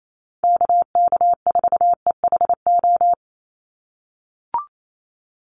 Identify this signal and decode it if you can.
Morse code, heard as a keyed tone, about 28 wpm, 700 Hz. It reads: KK4EHO
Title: USA Call Signs 28wpm